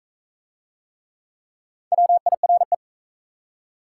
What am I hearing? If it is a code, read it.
Morse code, heard as a keyed tone, about 41 wpm, 700 Hz. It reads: WIRE